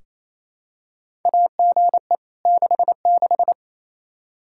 Morse code, heard as a keyed tone, about 28 wpm, 700 Hz. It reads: AGE 66
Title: Practice QSO - Exchange 3 - More Info - 28wpm